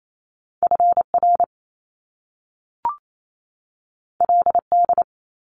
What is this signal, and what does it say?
Morse code, heard as a keyed tone, about 28 wpm, 700 Hz. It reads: FR LD
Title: Top Two Letter Combinations in Words 28wpm